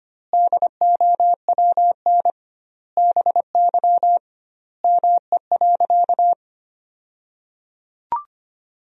Morse code, heard as a keyed tone, about 25 wpm, 700 Hz. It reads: DOWN BY ME.
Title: 2x Word-Spacing - Sentences from Top 500 Words - 25wpm